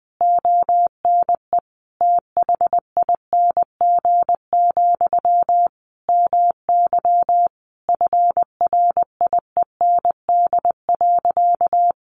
Morse code, heard as a keyed tone, about 20 wpm, 700 Hz. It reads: ONE THING, MY FRIEND.
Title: Sentences from Top 1200 Words 20wpm